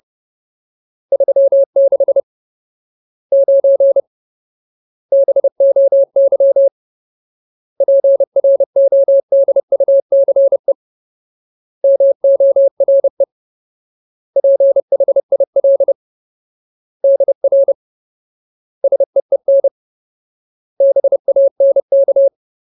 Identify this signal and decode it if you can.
Morse code, heard as a keyed tone, about 30 wpm, 550 Hz. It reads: 36 9 BOY PRODUCE MORE PHIL DR SEEN BANK
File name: Ultimate Word List / 30WPM Ultimate